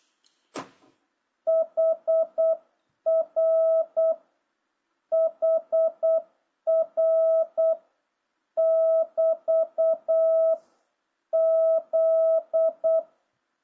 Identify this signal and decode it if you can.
Morse code, heard as a keyed tone, about 8 wpm, 660 Hz. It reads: HR HR <BT> Z